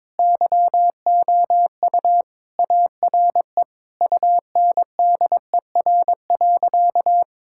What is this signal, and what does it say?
Morse code, heard as a keyed tone, about 22 wpm, 700 Hz. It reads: YOU ARE UNDER.
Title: Sentences from Top 1600 Words 22wpm